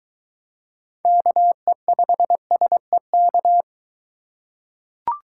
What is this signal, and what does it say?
Morse code, heard as a keyed tone, about 23 wpm, 700 Hz. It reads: KE5SEK